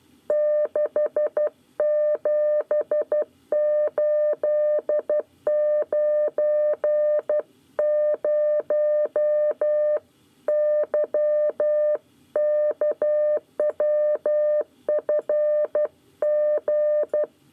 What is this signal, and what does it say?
Morse code, heard as a keyed tone, about 12 wpm, 600 Hz. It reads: 67890 YKWFG
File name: 9 lezione cw 1